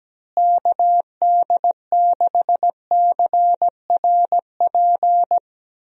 Morse code, heard as a keyed tone, about 17 wpm, 700 Hz. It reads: KD6CRP
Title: USA Call Signs - 6 Characters - Repeated 3x - 17wpm